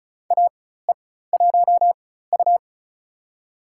Morse code, heard as a keyed tone, about 36 wpm, spaced as for 16 wpm, 700 Hz. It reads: AE1U
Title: USA Call Signs - 4x Character Spacing - 35wpm